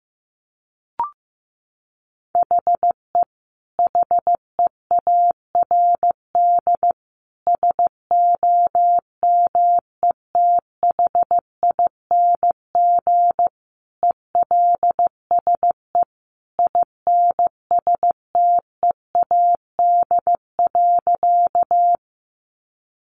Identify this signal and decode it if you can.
Morse code, heard as a keyed tone, about 15 wpm, 700 Hz. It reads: HE HEARD SOMETHING ELSE INSTEAD.
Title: Sentences from Top 800 Words 15wpm